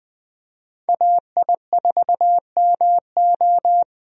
Morse code, heard as a keyed tone, about 21 wpm, 700 Hz. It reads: AI4MO
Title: USA Call Signs - 5 Characters - 20wpm